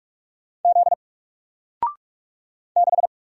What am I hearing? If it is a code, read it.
Morse code, heard as a keyed tone, about 44 wpm, 700 Hz. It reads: Z 6
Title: Single Character (Letter:Number:Punctuation:Prosign) Alt 45wpm